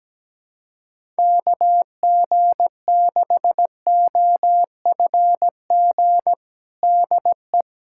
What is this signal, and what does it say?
Morse code, heard as a keyed tone, about 17 wpm, 700 Hz. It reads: KG6OFG DE
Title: USA Callsign DE Callsign 17wpm